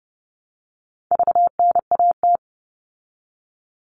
Morse code, heard as a keyed tone, about 30 wpm, 700 Hz. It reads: VNAT
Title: All Cut Numbers - 4-Digit 30wpm